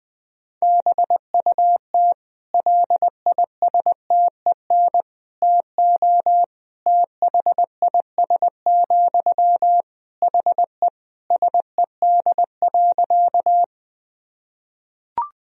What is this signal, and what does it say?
Morse code, heard as a keyed tone, about 21 wpm, 700 Hz. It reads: BUT LISTEN TO THIS, HE SED.
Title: Sentences from Top 400 Words and Abbreviations 20wpm